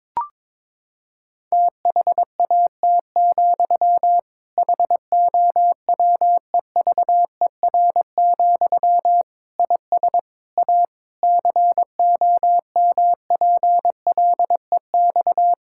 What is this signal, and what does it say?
Morse code, heard as a keyed tone, about 22 wpm, 700 Hz. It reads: THAT, HOWEVER, IS A COMPLEX
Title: Sentences from Top 1000 Words - Review Part 6 - 22wpm